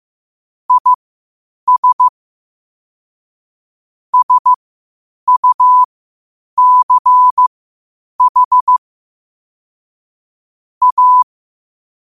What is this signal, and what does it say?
Morse code, heard as a keyed tone, about 15 wpm, spaced as for 8 wpm, 1000 Hz. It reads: IS SUCH A
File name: 8wpm